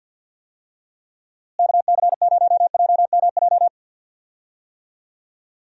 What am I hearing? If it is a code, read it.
Morse code, heard as a keyed tone, about 50 wpm, 700 Hz. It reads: KC0JMJ
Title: USA Call Signs - 6 Characters - Repeated 3x - 50wpm